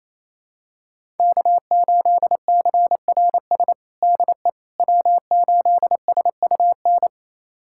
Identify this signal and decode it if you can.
Morse code, heard as a keyed tone, about 28 wpm, 700 Hz. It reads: K8CRS DE W8SUN
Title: USA Callsign DE Callsign - 5 Characters - 28wpm